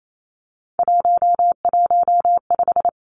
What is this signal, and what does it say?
Morse code, heard as a keyed tone, about 28 wpm, 700 Hz. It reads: 115
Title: Three Challenging Letter-Number Combinations 28wpm